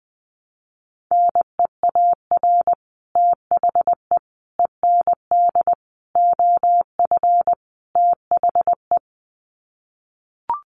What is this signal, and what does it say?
Morse code, heard as a keyed tone, about 20 wpm, 700 Hz. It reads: NEAR THE END OF THE
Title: Sets of 5 Words Encore 20wpm